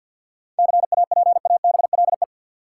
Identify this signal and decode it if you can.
Morse code, heard as a keyed tone, about 51 wpm, 700 Hz. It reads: CAPABLE
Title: Sentences from Top 1200 Words - New Word Drill - 50wpm